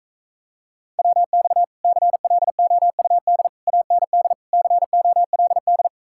Morse code, heard as a keyed tone, about 45 wpm, 700 Hz. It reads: WX CLOUD AND COLD